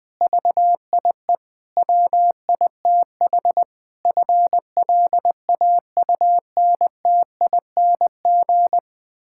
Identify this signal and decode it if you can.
Morse code, heard as a keyed tone, about 20 wpm, 700 Hz. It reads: VIE WITH FLAUNTING
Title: Pangram Challenge v2 - 20wpm